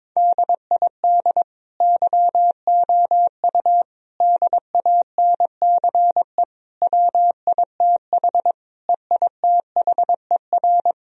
Morse code, heard as a keyed tone, about 21 wpm, 700 Hz. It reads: DID YOU DANCE WITH EITHER
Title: Sentences from Top 1000 Words - Review Part 4 - 22wpm